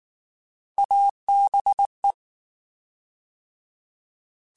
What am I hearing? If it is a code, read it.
Morse code, heard as a keyed tone, about 19 wpm, 800 Hz. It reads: ABE
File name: Alphabetical / Alphabetical mp3s / As @18